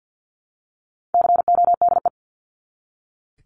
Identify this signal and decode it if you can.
Morse code, heard as a keyed tone, about 48 wpm, 700 Hz. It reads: CODE
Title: Top 1501-1750 Words 50wpm